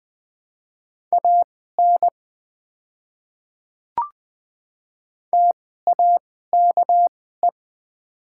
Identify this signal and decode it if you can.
Morse code, heard as a keyed tone, about 20 wpm, spaced as for 14 wpm, 690 Hz. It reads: AN TAKE
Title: 2x Character Spacing - Top 200 Words - 20wpm